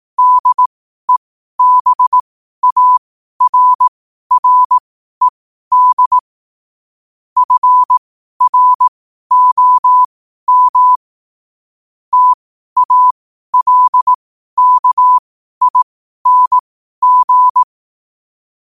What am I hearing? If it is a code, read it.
Morse code, heard as a keyed tone, about 18 wpm, spaced as for 12 wpm, 1000 Hz. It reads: DEBARRED FROM TALKING
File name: qotdfiles / 12wpm